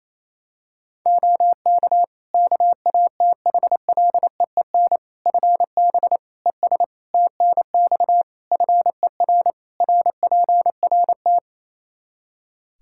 Morse code, heard as a keyed tone, about 28 wpm, 700 Hz. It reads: OK KATHLEEN FB ES TNX FER RPRT
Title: Practice QSO - Exchange 2 - Info - 28wpm